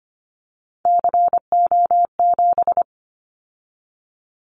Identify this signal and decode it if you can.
Morse code, heard as a keyed tone, about 25 wpm, 700 Hz. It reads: CO7
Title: Callsign Prefixes for Top 100 DXCC - 25wpm